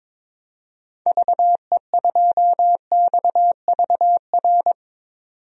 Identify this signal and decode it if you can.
Morse code, heard as a keyed tone, about 22 wpm, 700 Hz. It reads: VE2XVR
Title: DXCC-100 Call Signs - 6 Characters - 22wpm